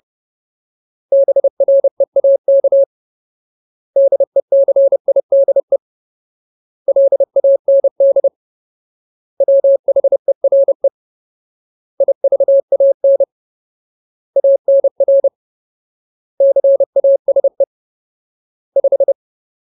Morse code, heard as a keyed tone, about 30 wpm, 550 Hz. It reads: BREAK DECIDE LAND WHERE IVAN ANR CASE 5
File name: Ultimate Word List / 30WPM Ultimate